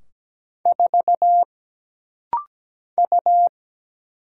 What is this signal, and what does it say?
Morse code, heard as a keyed tone, about 17 wpm, 700 Hz. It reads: 4 U
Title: Single Letter-Number - Rapid-Fire 17wpm